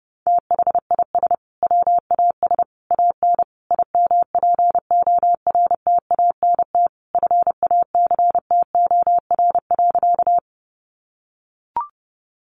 Morse code, heard as a keyed tone, about 30 wpm, 700 Hz. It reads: THIS WAS AN IMPORTANT FACTOR.
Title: Sentences from Top 1300 Words 30wpm